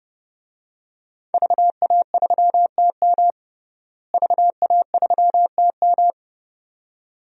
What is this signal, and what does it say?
Morse code, heard as a keyed tone, about 30 wpm, 700 Hz. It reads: VA3TM VA3TM
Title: DXCC-100 Call Signs - 5 Characters - Repeated 2x - 30wpm